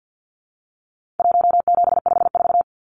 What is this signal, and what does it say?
Morse code, heard as a keyed tone, about 48 wpm, 700 Hz. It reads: 1754
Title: Numbers - Contest Style - 4-Digit 50wpm